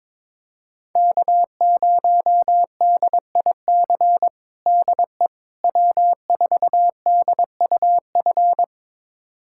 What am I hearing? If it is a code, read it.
Morse code, heard as a keyed tone, about 21 wpm, 700 Hz. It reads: K0DIC DE W4DUF
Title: USA Callsign DE Callsign - 5 Characters - 22wpm